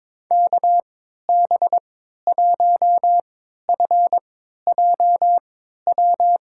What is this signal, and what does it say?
Morse code, heard as a keyed tone, about 23 wpm, spaced as for 13 wpm, 700 Hz. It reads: KB1FJW
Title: USA Call Signs - 6 Characters - 3x Character Spacing - 22wpm